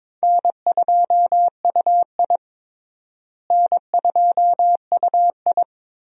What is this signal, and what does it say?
Morse code, heard as a keyed tone, about 22 wpm, 700 Hz. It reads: N2UI N2UI
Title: USA Call Signs - 4 Characters - Repeated 3x - 22wpm